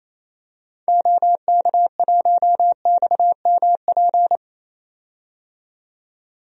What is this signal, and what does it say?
Morse code, heard as a keyed tone, about 28 wpm, 700 Hz. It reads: OK1XMP
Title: DXCC-100 Call Signs - 6 Characters - 28wpm